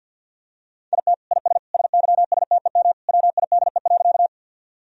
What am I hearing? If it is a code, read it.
Morse code, heard as a keyed tone, about 50 wpm, 700 Hz. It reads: IT IS SYSTEM WIDE.
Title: Sentences from Top 1000 Words - Review Part 3 - 50wpm